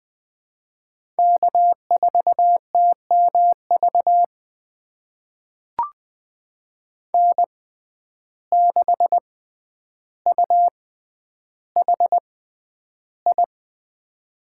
Morse code, heard as a keyed tone, about 20 wpm, spaced as for 10 wpm, 700 Hz. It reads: K4TMV N6UHI
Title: USA Call Signs - 5 Characters - 6x Character Spacing - 20wpm